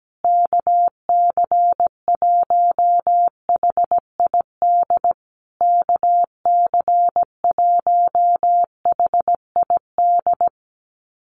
Morse code, heard as a keyed tone, about 17 wpm, 700 Hz. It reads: KC1HID KC1HID
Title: USA Call Signs Repeated 3x 17wpm